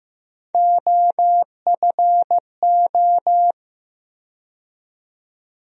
Morse code, heard as a keyed tone, about 15 wpm, 700 Hz. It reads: OFO
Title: USA Call Sign Postfixes - 3 Letters 15wpm